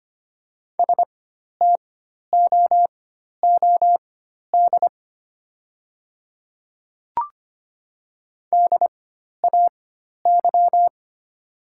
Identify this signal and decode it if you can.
Morse code, heard as a keyed tone, about 25 wpm, spaced as for 12 wpm, 700 Hz. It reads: STOOD DAY
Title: 4x Character Spacing - Top 500 Words - 25wpm